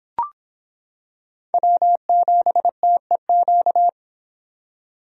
Morse code, heard as a keyed tone, about 25 wpm, 700 Hz. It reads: W7TEQ